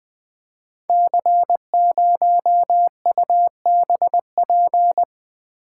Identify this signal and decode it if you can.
Morse code, heard as a keyed tone, about 20 wpm, 700 Hz. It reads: C0UBP
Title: Code Groups - 5 Letter:Numbers - 20wpm